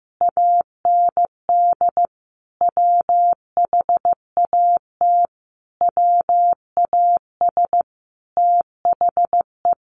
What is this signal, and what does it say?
Morse code, heard as a keyed tone, about 15 wpm, 700 Hz. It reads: AND WHAT WAS THE
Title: Sentences from Top 1500 Words 15wpm